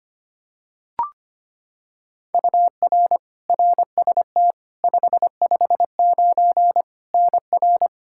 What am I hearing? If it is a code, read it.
Morse code, heard as a keyed tone, about 25 wpm, 700 Hz. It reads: UR RST 559 NR
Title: Practice QSO - Exchange 1 - Beginning - 25wpm